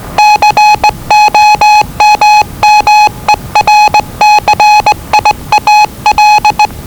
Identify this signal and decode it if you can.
Morse code, heard as a keyed tone, about 20 wpm, 840 Hz. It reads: COMMERCIAL
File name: WellingtonRadioZlwCloseDownOnHfAndMf / 01-Zlw 4277khz 417.5 qrt until 0925z